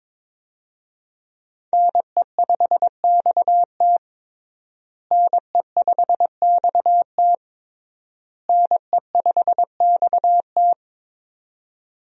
Morse code, heard as a keyed tone, about 22 wpm, 700 Hz. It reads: NE5XT NE5XT NE5XT
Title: USA Call Signs - 5 Characters - Repeated 3x - 22wpm